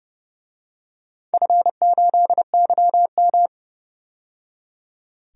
Morse code, heard as a keyed tone, about 31 wpm, 700 Hz. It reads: F8YM
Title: DXCC-100 Call Signs - 4 Characters - 30wpm